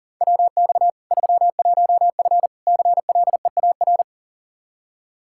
Morse code, heard as a keyed tone, about 40 wpm, 700 Hz. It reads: WX 31F CLEAR
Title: Practice QSO - Exchange 2 - Info - 40wpm